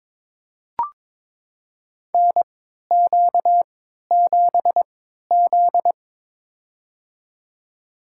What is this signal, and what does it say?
Morse code, heard as a keyed tone, about 22 wpm, spaced as for 13 wpm, 700 Hz. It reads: NQ7Z
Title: USA Call Signs - 4 Characters - 3x Character Spacing - 22wpm